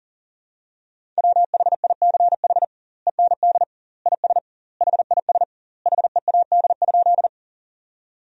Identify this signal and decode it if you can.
Morse code, heard as a keyed tone, about 40 wpm, 700 Hz. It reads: WHICH END IS HIS HEAD?
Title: 2x Word-Spacing - Sentences from Top 300 Words - 40wpm